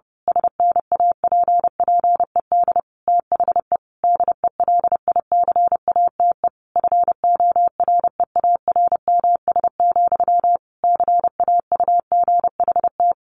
Morse code, heard as a keyed tone, about 30 wpm, 700 Hz. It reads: SNAPPED THE DELICATE FOREARMS, CAUGHT
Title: A Sound Of Thunder 30wpm